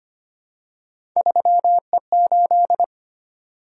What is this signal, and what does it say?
Morse code, heard as a keyed tone, about 25 wpm, 700 Hz. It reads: 3E8
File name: Three Letter-Number Combinations - 25wpm